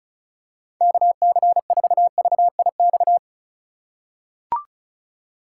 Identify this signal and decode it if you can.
Morse code, heard as a keyed tone, about 34 wpm, 700 Hz. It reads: KC4VIX
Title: USA Call Signs - 6 Characters - 2x Character Spacing - 35wpm